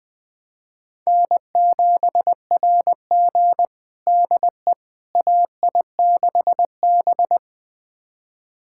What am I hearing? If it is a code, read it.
Morse code, heard as a keyed tone, about 20 wpm, 700 Hz. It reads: N7RG DE AI6B
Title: USA Callsign DE Callsign - 4 Characters - 20wpm